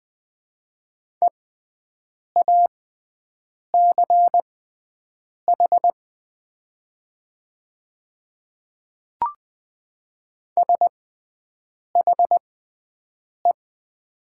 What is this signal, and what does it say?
Morse code, heard as a keyed tone, about 20 wpm, spaced as for 7 wpm, 700 Hz. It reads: EACH SHE
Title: 6x Character Spacing - Top 200 Words - 20wpm